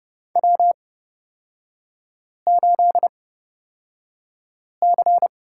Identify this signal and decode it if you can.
Morse code, heard as a keyed tone, about 29 wpm, spaced as for 5 wpm, 700 Hz. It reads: W8C
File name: Lesson 36e - BT - Three Characters 30wpm